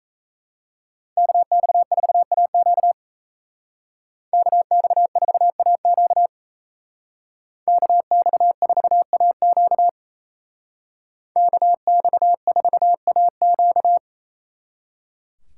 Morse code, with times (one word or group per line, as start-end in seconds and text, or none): 1.2-2.9 KX4AQ
4.3-6.3 KX4AQ
7.7-9.9 KX4AQ
11.4-14.0 KX4AQ